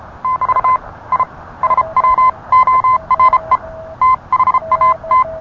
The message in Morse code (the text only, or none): <BT> I SWORE THAT